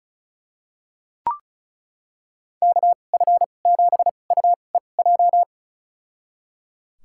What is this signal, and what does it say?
Morse code, heard as a keyed tone, about 36 wpm, spaced as for 25 wpm, 700 Hz. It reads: KF7UEJ
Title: USA Call Signs - 6 Characters - 2x Character Spacing - 35wpm